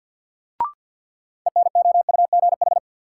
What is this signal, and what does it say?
Morse code, heard as a keyed tone, about 51 wpm, 700 Hz. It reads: ENOUGH
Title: Sets of 2 Words - New Word Drill - 50wpm